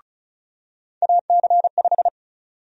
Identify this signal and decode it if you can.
Morse code, heard as a keyed tone, about 34 wpm, 700 Hz. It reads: AC5
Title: USA Call Sign Prefixes - 35wpm